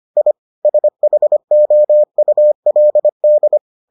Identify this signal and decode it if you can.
Morse code, heard as a keyed tone, about 25 wpm, 600 Hz. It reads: I SHOULD